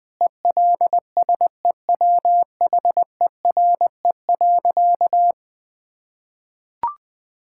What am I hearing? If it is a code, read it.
Morse code, heard as a keyed tone, about 21 wpm, 700 Hz. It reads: ELSEWHERE.
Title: Sentences from Top 1600 Words 20wpm